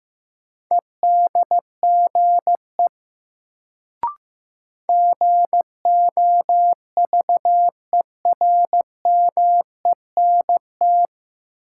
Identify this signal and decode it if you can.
Morse code, heard as a keyed tone, about 15 wpm, 700 Hz. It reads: EDGE GOVERMENT
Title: Sentences from Top 1300 Words - New Word Drill - 15wpm